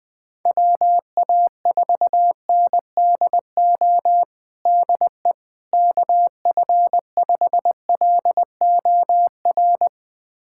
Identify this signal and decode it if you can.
Morse code, heard as a keyed tone, about 20 wpm, 700 Hz. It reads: WA4NDO DE KF5LOR
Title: USA Callsign DE Callsign 20wpm